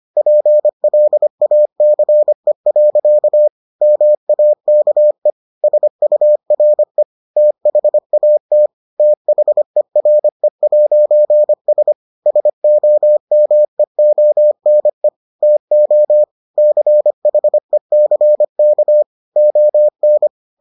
Morse code, heard as a keyed tone, about 25 wpm, 600 Hz. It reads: PLACE. MAKE SURE THAT THERE'S SOMEONE TO CHECK ON